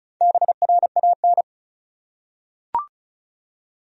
Morse code, heard as a keyed tone, about 34 wpm, 700 Hz. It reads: BRAN
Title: USA Common Names - Max 4 Letters - 35wpm